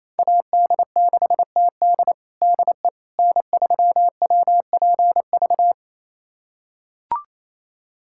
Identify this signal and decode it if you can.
Morse code, heard as a keyed tone, about 28 wpm, 700 Hz. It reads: AD6TD DE N3WPV
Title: USA Callsign DE Callsign - 5 Characters - 28wpm